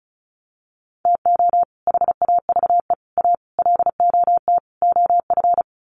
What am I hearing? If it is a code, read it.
Morse code, heard as a keyed tone, about 34 wpm, 700 Hz. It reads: TO HAVE A LOT OF
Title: Sets of 5 Words Encore 35wpm